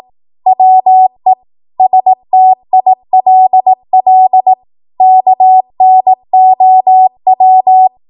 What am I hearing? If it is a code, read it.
Morse code, heard as a keyed tone, about 18 wpm, 750 Hz. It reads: WE STILL KNOW
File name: Morse / 18WPM